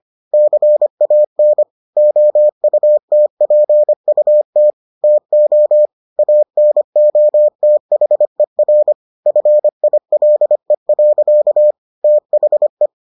Morse code, heard as a keyed tone, about 25 wpm, 600 Hz. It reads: CAN OUTPUT TO ANOTHER FILE. THE